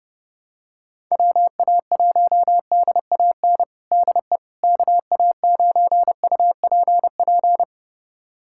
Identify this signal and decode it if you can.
Morse code, heard as a keyed tone, about 30 wpm, 700 Hz. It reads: WA1DAN DE KA9UPP
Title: USA Callsign DE Callsign 30wpm